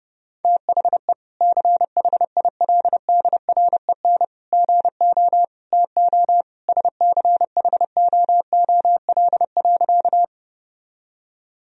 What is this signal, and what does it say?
Morse code, heard as a keyed tone, about 32 wpm, 700 Hz. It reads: THE CHILDREN GO TO SCHOOL.
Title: Sentences from Top 1000 Words - Review Part 1 - 30wpm